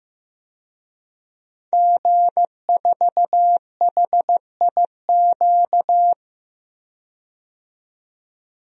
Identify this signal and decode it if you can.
Morse code, heard as a keyed tone, about 15 wpm, 700 Hz. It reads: G4HIQ
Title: DXCC-100 Call Signs - 5 Characters - 15wpm